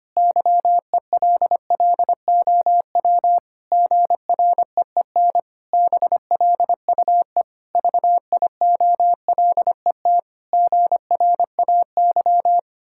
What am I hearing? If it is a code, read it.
Morse code, heard as a keyed tone, about 25 wpm, 700 Hz. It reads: YELLOW GREEN BLUE VIOLET GRAY
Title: Pangram Challenge v2 - 25wpm